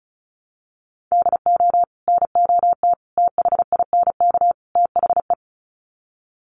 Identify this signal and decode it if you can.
Morse code, heard as a keyed tone, about 34 wpm, 700 Hz. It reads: DO NOT THINK THE
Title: Sets of 4 Words 35wpm